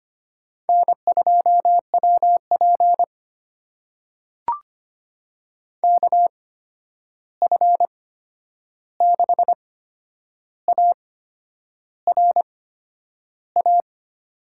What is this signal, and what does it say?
Morse code, heard as a keyed tone, about 25 wpm, spaced as for 9 wpm, 700 Hz. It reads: N2WP KF6ARA
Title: USA Call Signs - 8x Character Spacing - 25wpm